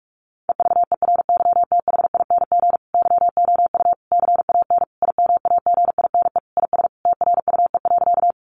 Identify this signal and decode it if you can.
Morse code, heard as a keyed tone, about 44 wpm, 700 Hz. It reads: EVERYTHING YOU CAN IMAGINE IS TRUE.
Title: Sentences from Top 1400 Words 45wpm